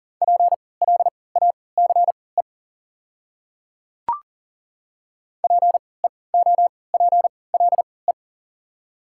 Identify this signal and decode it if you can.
Morse code, heard as a keyed tone, about 40 wpm, spaced as for 23 wpm, 690 Hz. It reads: PLACE PEOPLE
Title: 3x Character Spacing - Top 300 Words - 40wpm